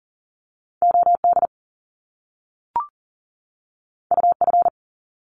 Morse code, heard as a keyed tone, about 38 wpm, 700 Hz. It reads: OD UF
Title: Top Two Letter Combinations in Words 40wpm